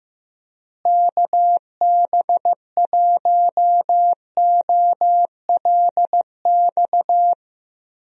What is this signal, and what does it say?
Morse code, heard as a keyed tone, about 15 wpm, 700 Hz. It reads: KB1OLX
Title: USA Call Signs - 6 Characters - Repeated 2x - 15wpm